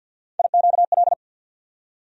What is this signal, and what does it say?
Morse code, heard as a keyed tone, about 52 wpm, 700 Hz. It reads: IQL
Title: USA Call Sign Postfixes - 3 Letters 50wpm